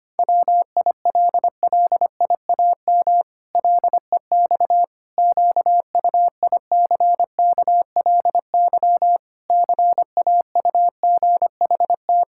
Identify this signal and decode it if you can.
Morse code, heard as a keyed tone, about 25 wpm, 700 Hz. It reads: WILLIAM LEX QUICKLY CAUGHT
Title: Pangram Challenge v2 - 25wpm